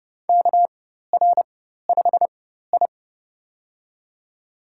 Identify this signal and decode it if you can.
Morse code, heard as a keyed tone, about 30 wpm, spaced as for 14 wpm, 700 Hz. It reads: KR5I
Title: USA Call Signs - 4 Characters - 4x Character Spacing - 30wpm